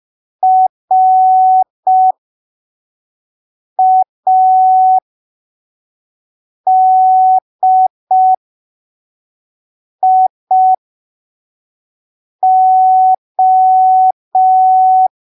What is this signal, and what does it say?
Morse code, heard as a keyed tone, about 5 wpm, spaced as for 3 wpm, 750 Hz. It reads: RADIO
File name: arquivos / texto / texto 13 05ppm